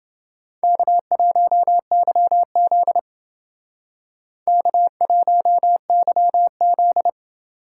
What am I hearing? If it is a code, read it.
Morse code, heard as a keyed tone, about 28 wpm, 700 Hz. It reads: K1YZ K1YZ